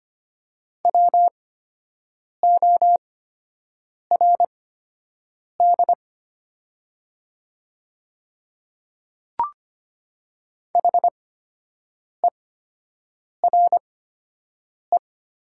Morse code, heard as a keyed tone, about 25 wpm, spaced as for 7 wpm, 700 Hz. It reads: WORD HERE
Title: 8x Character Spacing - Top 300 Words - 25wpm